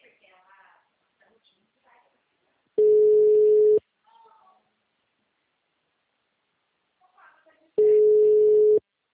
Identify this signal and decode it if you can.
Ringback tone (Europe, 425 Hz), with speech.